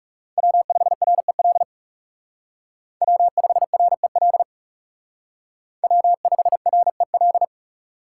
Morse code, heard as a keyed tone, about 34 wpm, 700 Hz. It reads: W5REL W5REL W5REL
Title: Speed Racing - USA 1x3 Call Signs 30wpm